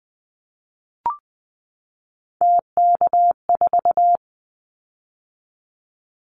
Morse code, heard as a keyed tone, about 20 wpm, 700 Hz. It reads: TK4
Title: Callsign Prefixes for Top 100 DXCC - 20wpm